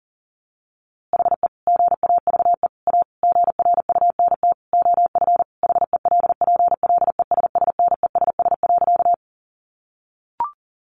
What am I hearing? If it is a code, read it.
Morse code, heard as a keyed tone, about 40 wpm, 700 Hz. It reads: HE GAVE A GRUNT OF HELPLESSNESS.
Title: A Sound Of Thunder 40wpm